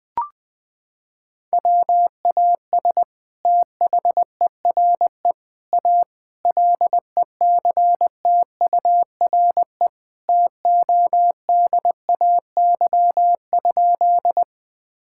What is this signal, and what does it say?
Morse code, heard as a keyed tone, about 20 wpm, 700 Hz. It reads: WAS THERE A LECTURE TODAY?
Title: Sentences from Top 1600 Words 20wpm